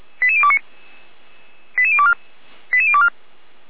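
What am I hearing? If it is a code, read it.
Five-tone selective call (ZVEI), heard as quick runs of five tones, tones of about 70 ms. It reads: 88029 88024 88024
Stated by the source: Five-tone ZVEI selective call received by SDR.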